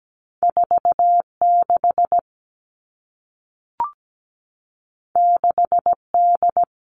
Morse code, heard as a keyed tone, about 17 wpm, 700 Hz. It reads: 46 6D
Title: Two Challenging Letter-Number Combinations 17wpm